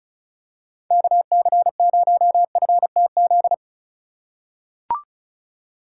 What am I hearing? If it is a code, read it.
Morse code, heard as a keyed tone, about 36 wpm, 700 Hz. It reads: KC0FTZ
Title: USA Call Signs - 6 Characters - 4x Character Spacing - 35wpm